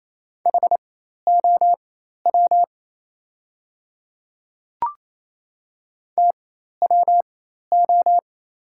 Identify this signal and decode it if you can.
Morse code, heard as a keyed tone, about 28 wpm, spaced as for 13 wpm, 700 Hz. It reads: HOW TWO
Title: 4x Character Spacing - Top 100 Words - 28wpm